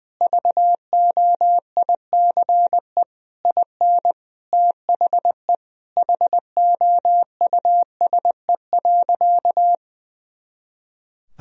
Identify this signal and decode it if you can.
Morse code, heard as a keyed tone, about 20 wpm, 700 Hz. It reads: VOICE IN THE HOUSE.
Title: Sentences from Top 500 Words and Abbreviations 20wpm